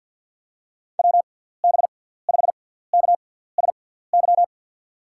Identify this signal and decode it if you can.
Morse code, heard as a keyed tone, about 50 wpm, spaced as for 17 wpm, 700 Hz. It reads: WB5KSY
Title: USA Call Signs - 6 Characters - 6x Character Spacing - 50wpm